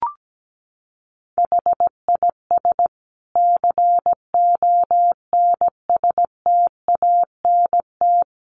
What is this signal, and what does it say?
Morse code, heard as a keyed tone, about 17 wpm, 700 Hz. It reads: HIS CONSTANT
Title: Sentences from Top 1500 Words 17wpm